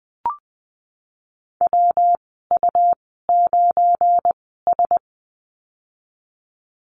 Morse code, heard as a keyed tone, about 19 wpm, spaced as for 14 wpm, 700 Hz. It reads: WU9S
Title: 179 Lesson 29g - Slash - Call Signs 20wpm